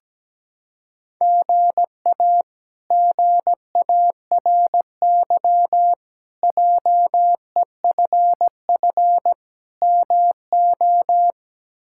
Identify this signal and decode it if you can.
Morse code, heard as a keyed tone, about 17 wpm, 700 Hz. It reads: GA GARY JEFF MO
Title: K1USN SST QSO Practice - 17wpm